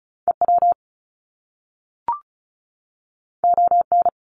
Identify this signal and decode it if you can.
Morse code, heard as a keyed tone, about 36 wpm, 700 Hz. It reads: EW ON
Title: Top Two Letter Combinations in Words 35wpm